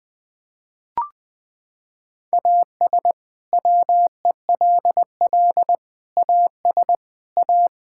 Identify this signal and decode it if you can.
Morse code, heard as a keyed tone, about 21 wpm, 700 Hz. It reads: AS WELL AS A
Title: Sets of 6 Words - 20wpm